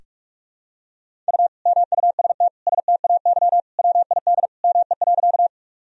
Morse code, heard as a keyed tone, about 43 wpm, 700 Hz. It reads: U MUST STAY WID ME.
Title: Sentences from Top 500 Words and Abbreviations 45wpm